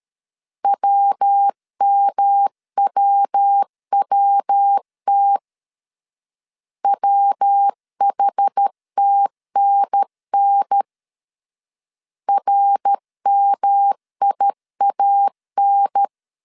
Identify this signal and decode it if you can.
Morse code, heard as a keyed tone, about 13 wpm, 800 Hz. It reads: WMWWT WHTNN RMIAN
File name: CW / DET / 1 3 1500C S A 12wpm 2905